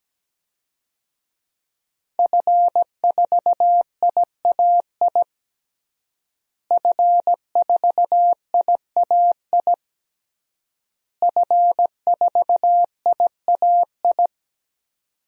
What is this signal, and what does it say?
Morse code, heard as a keyed tone, about 17 wpm, 700 Hz. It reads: F4IAI F4IAI F4IAI
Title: DXCC-100 Call Signs - 5 Characters - Repeated 3x - 17wpm